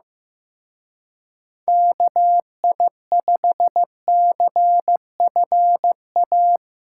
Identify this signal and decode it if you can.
Morse code, heard as a keyed tone, about 17 wpm, 700 Hz. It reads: KI5CFA
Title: Speed Racing - USA 2x3 Call Signs 15wpm